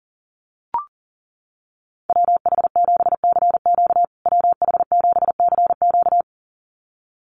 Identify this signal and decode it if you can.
Morse code, heard as a keyed tone, about 40 wpm, 700 Hz. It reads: WH7CQ WH7CQ
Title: USA Call Signs Repeated 2x 40wpm